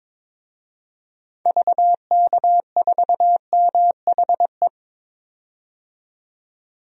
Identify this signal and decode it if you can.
Morse code, heard as a keyed tone, about 22 wpm, 700 Hz. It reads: VK4MHE
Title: DXCC-100 Call Signs - 6 Characters - 22wpm